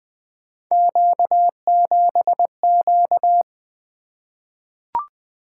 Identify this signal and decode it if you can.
Morse code, heard as a keyed tone, about 20 wpm, 700 Hz. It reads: Q7Q
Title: Three Letter-Number Combinations - 20wpm